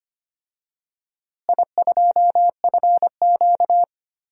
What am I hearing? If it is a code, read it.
Morse code, heard as a keyed tone, about 25 wpm, 700 Hz. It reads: I2FQ
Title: DXCC-100 Call Signs - 4 Characters - Repeated 3x - 25wpm